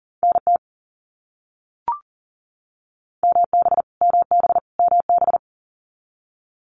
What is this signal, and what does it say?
Morse code, heard as a keyed tone, about 40 wpm, 700 Hz. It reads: NT MB MB MB
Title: Canadian Provinces and Territories - Repeated 3x - 40wpm